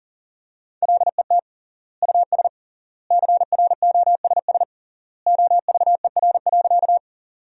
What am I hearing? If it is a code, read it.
Morse code, heard as a keyed tone, about 40 wpm, 700 Hz. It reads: LET US CROSS OVER.
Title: 3x Word-Spacing - Sentences from Top 300 Words - 40wpm